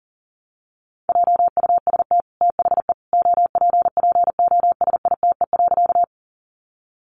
Morse code, heard as a keyed tone, about 40 wpm, 700 Hz. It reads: JUST THE OPPOSITE.
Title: Sentences from Top 1300 Words 40wpm